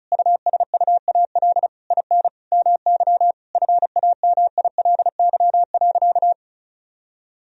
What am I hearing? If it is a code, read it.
Morse code, heard as a keyed tone, about 34 wpm, 700 Hz. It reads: USUAL IN MY FAMILY.
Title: Sentences from Top 1000 Words - Review Part 2 - 35wpm